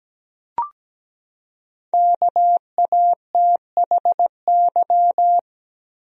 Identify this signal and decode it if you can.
Morse code, heard as a keyed tone, about 17 wpm, 700 Hz. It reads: KATHY
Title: USA Common Names - 17wpm